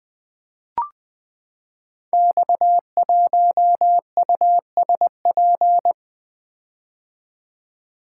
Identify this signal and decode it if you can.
Morse code, heard as a keyed tone, about 20 wpm, 700 Hz. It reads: X1USP
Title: Code Groups - 5 Letter:Numbers - 20wpm